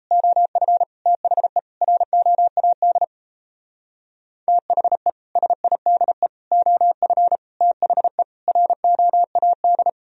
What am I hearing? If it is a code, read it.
Morse code, heard as a keyed tone, about 34 wpm, 700 Hz. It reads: OF THE ROAD THE SIDE OF THE ROAD
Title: Speed-Racing - Sets of 5 Words 28wpm